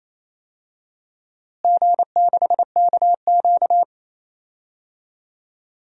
Morse code, heard as a keyed tone, about 28 wpm, 700 Hz. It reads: G6KQ
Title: DXCC-100 Call Signs - 4 Characters - Repeated 3x - 28wpm